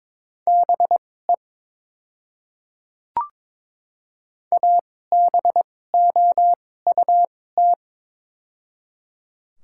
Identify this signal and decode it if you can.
Morse code, heard as a keyed tone, about 22 wpm, spaced as for 16 wpm, 700 Hz. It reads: BE ABOUT
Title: 2x Character Spacing - Top 300 Words - 22wpm